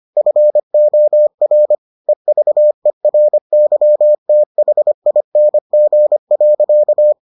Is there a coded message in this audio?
Yes: Morse code from 0.2 to 7.1 s, heard as a keyed tone, about 25 wpm, 600 Hz. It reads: FOR EVERYTHING.